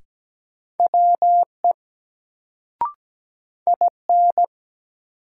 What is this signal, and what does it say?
Morse code, heard as a keyed tone, about 17 wpm, 700 Hz. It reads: WE IN
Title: Sentences from Top 100 Words - New Word Drill - 17wpm